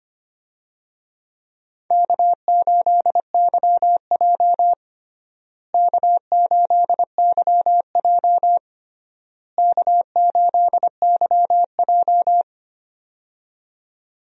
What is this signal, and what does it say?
Morse code, heard as a keyed tone, about 25 wpm, 700 Hz. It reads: K8YJ K8YJ K8YJ
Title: USA Call Signs - 4 Characters - Repeated 3x - 25wpm